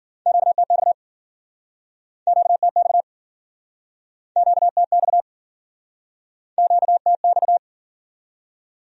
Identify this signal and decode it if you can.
Morse code, heard as a keyed tone, about 56 wpm, 700 Hz. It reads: QTX QTX QTX QTX